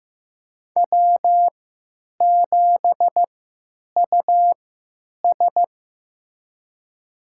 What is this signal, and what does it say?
Morse code, heard as a keyed tone, about 15 wpm, spaced as for 9 wpm, 700 Hz. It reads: W7US
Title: USA Call Signs - 4 Characters - 3x Character Spacing - 15wpm